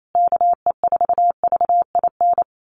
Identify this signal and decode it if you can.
Morse code, heard as a keyed tone, about 28 wpm, 700 Hz. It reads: KE4VIN
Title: USA Call Signs Repeated 3x 28wpm